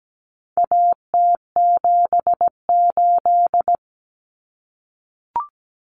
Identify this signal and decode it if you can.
Morse code, heard as a keyed tone, about 17 wpm, 700 Hz. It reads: AT78
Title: Common Cut Numbers - 4-Digit 17wpm